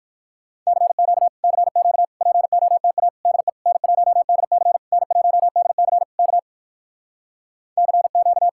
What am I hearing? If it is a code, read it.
Morse code, heard as a keyed tone, about 59 wpm, 700 Hz. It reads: CQ CQ POTA DE N1DC N1DC K CQ